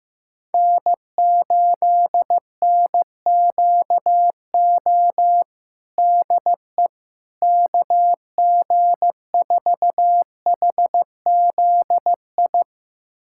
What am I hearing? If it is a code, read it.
Morse code, heard as a keyed tone, about 16 wpm, 700 Hz. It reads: N8NQO DE KG4HZI
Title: USA Callsign DE Callsign 15wpm